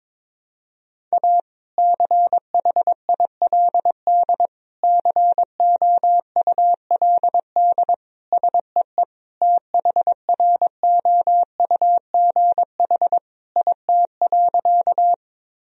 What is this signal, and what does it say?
Morse code, heard as a keyed tone, about 21 wpm, 700 Hz. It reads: A CHILD COULD SEE THROUGH IT.
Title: Sentences from Top 1000 Words - Review Part 1 - 22wpm